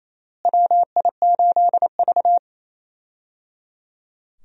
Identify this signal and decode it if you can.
Morse code, heard as a keyed tone, about 28 wpm, 700 Hz. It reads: WI8V
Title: USA Call Signs - 4 Characters - Repeated 3x - 28wpm